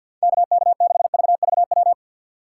Morse code, heard as a keyed tone, about 48 wpm, 700 Hz. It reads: KK6VVW
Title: USA Call Signs - 6 Characters - 6x Character Spacing - 50wpm